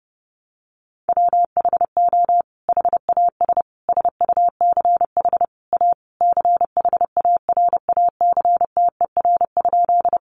Morse code, heard as a keyed tone, about 30 wpm, 700 Hz. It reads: WHO HAS SUCH A CHARACTER?
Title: Sentences from Top 1200 Words 30wpm